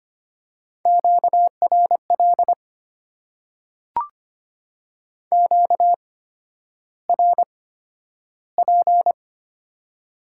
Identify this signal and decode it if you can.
Morse code, heard as a keyed tone, about 25 wpm, spaced as for 11 wpm, 690 Hz. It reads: QRL QRP
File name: Top 90 QSO Elements - 8x Character Spacing - 25wpm